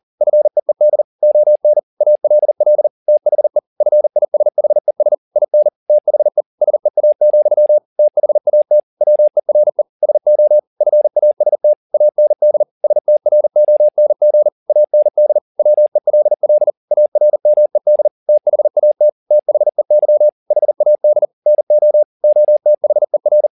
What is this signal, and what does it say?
Morse code, heard as a keyed tone, about 40 wpm, 600 Hz. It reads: FEED ON ALL THE FISHES IN THE SEA, THAT WERE SO FAST AND STRONG AND WELL ARMED THAT THEY HAD NO OTHER
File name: the old man and the sea 2 40wpm